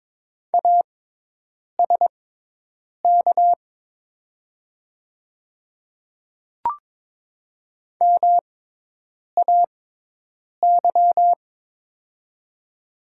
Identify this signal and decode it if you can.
Morse code, heard as a keyed tone, about 22 wpm, spaced as for 8 wpm, 700 Hz. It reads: ASK MAY
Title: 6x Character Spacing - Top 200 Words - 22wpm